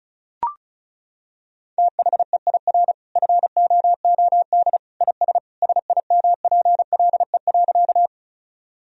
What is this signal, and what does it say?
Morse code, heard as a keyed tone, about 34 wpm, 700 Hz. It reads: THEIR FOOD IS SIMPLE.
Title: Sentences from Top 1000 Words - Review Part 1 - 35wpm